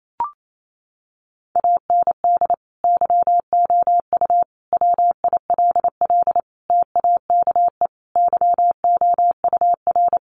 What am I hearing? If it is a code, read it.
Morse code, heard as a keyed tone, about 28 wpm, 700 Hz. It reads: AND YOU WILL TAKE YOUR
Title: Sentences from Top 900 Words 28wpm